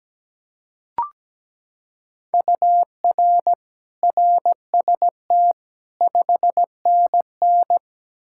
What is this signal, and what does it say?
Morse code, heard as a keyed tone, about 17 wpm, 700 Hz. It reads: UR RST 5NN
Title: K1USN SST QSO Practice 17wpm